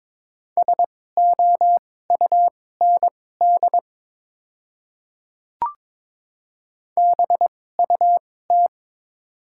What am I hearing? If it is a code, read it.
Morse code, heard as a keyed tone, about 22 wpm, spaced as for 16 wpm, 700 Hz. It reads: SOUND BUT
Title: 2x Character Spacing - Top 100 Words - 22wpm